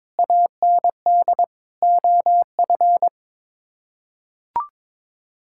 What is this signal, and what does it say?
Morse code, heard as a keyed tone, about 22 wpm, 700 Hz. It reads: AND OF
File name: Sets of 9 Words - 22wpm